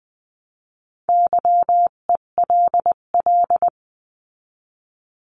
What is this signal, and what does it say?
Morse code, heard as a keyed tone, about 20 wpm, 700 Hz. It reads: YELL
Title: 258 Lesson 42c - Words 16-20wpm